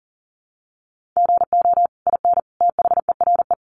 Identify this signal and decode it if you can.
Morse code, heard as a keyed tone, about 40 wpm, 700 Hz. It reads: GO IN THERE
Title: Sentences from Top 500 Words 40wpm